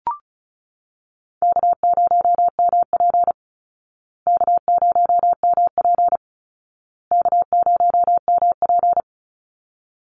Morse code, heard as a keyed tone, about 34 wpm, 700 Hz. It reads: K0MP K0MP K0MP
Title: Call Signs Repeated 2x - 35wpm